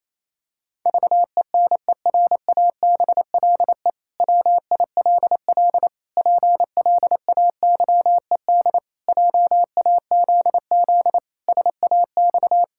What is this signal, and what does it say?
Morse code, heard as a keyed tone, about 28 wpm, 700 Hz. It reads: VENERABLE WILL PLAYED JAZZ SAX
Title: Pangram Challenge v2 - 28wpm